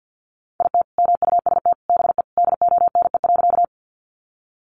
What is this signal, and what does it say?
Morse code, heard as a keyed tone, about 50 wpm, 700 Hz. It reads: IT MUST BE DONE.
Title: Sentences from Top 500 Words 50wpm